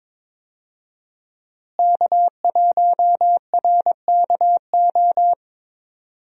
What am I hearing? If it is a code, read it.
Morse code, heard as a keyed tone, about 22 wpm, 700 Hz. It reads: K1RKO
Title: USA Call Signs - 5 Characters - Repeated 2x - 22wpm